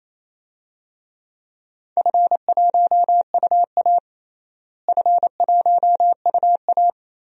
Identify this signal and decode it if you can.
Morse code, heard as a keyed tone, about 28 wpm, 700 Hz. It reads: F1UA F1UA
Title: DXCC-100 Call Signs - 4 Characters - Repeated 2x - 28wpm